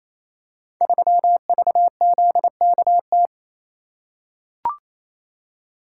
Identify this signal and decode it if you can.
Morse code, heard as a keyed tone, about 29 wpm, 700 Hz. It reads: 3VZKT
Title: Code Groups - 5 Letter:Numbers - 28wpm